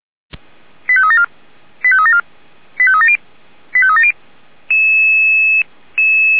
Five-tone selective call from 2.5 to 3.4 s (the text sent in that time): 85370